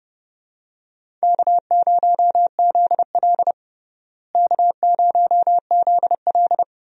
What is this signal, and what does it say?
Morse code, heard as a keyed tone, about 30 wpm, 700 Hz. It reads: K0ZL K0ZL
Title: USA Call Signs - 4 Characters - Repeated 2x - 30wpm